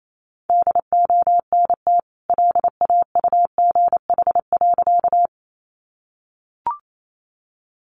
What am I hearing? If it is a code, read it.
Morse code, heard as a keyed tone, about 28 wpm, 700 Hz. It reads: DONT LAUGH.
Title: Sentences from Top 500 Words 28wpm